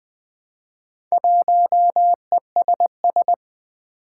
Morse code, heard as a keyed tone, about 20 wpm, 700 Hz. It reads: 1ESS
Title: Four Challenging Letter-Number Combinations 20wpm